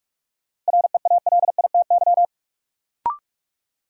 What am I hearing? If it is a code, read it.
Morse code, heard as a keyed tone, about 46 wpm, 700 Hz. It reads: REALITY
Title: Sentences from Top 1200 Words - New Word Drill - 45wpm